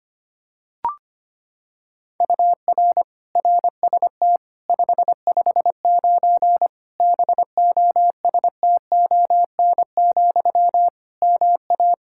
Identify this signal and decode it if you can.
Morse code, heard as a keyed tone, about 25 wpm, 700 Hz. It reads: UR RST 559 BOSTON, MA
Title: K1USN SST QSO Practice 25wpm